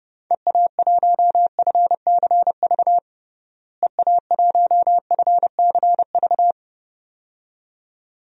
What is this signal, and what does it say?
Morse code, heard as a keyed tone, about 30 wpm, 700 Hz. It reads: EA1FCV EA1FCV
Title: DXCC-100 Call Signs - 6 Characters - Repeated 2x - 30wpm